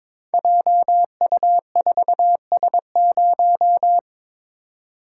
Morse code, heard as a keyed tone, about 22 wpm, 700 Hz. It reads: JU4S0
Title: Code Groups - 5 Letter:Numbers - 22wpm